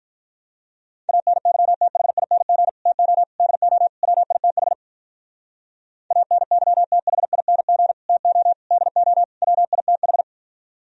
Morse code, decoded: ANYTHING TO DO WITH ANYTHING TO DO WITH

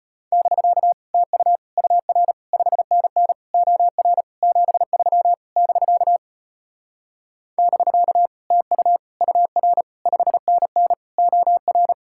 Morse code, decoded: <BK> TU UR 5NN OR 73 <BK> <BK> TU UR 5NN OR

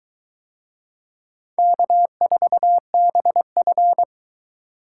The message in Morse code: K4BF

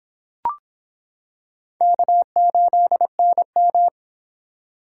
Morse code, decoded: K8NM

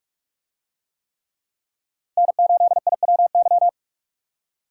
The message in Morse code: N8IWY

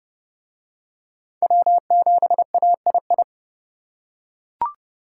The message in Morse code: W7AII

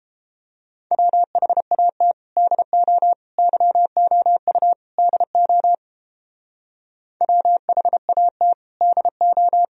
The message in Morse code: WHAT DO YOU DO WHAT DO

